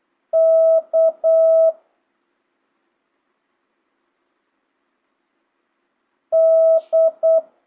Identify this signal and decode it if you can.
Morse code, heard as a keyed tone, about 8 wpm, 650 Hz. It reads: K D
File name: jh8cbh / 2025cw